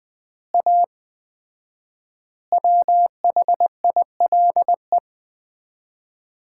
Morse code, heard as a keyed tone, about 21 wpm, 700 Hz. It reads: A WHILE
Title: 4x Word-Spacing - Sets of 3 Words - 20wpm